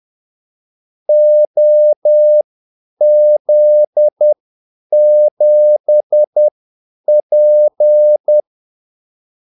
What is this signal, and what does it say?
Morse code, heard as a keyed tone, about 10 wpm, spaced as for 8 wpm, 600 Hz. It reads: OZ7P
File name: Callsigns-10-8wpm